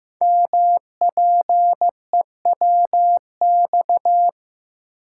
Morse code, heard as a keyed tone, about 15 wpm, 700 Hz. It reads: MPEWX
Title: Code Groups - 5 Letters - 15wpm